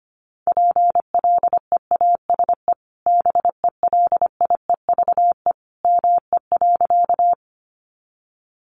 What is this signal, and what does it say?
Morse code, heard as a keyed tone, about 25 wpm, 700 Hz. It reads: PLEASE BELIEVE ME.
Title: Sentences from Top 1500 Words 25wpm